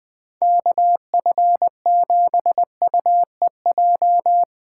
Morse code, heard as a keyed tone, about 20 wpm, 700 Hz. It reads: KF7UEJ
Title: USA Call Signs - 6 Characters - 4x Character Spacing - 20wpm